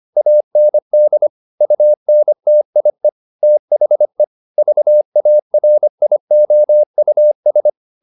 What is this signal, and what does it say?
Morse code, heard as a keyed tone, about 25 wpm, 600 Hz. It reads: AND UNTIE THE VARIOUS
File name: the old man and the sea part 1 25wpm